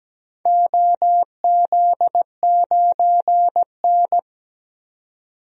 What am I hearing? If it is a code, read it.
Morse code, heard as a keyed tone, about 17 wpm, 700 Hz. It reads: OZ9N
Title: DXCC-100 Call Signs - 4 Characters - 17wpm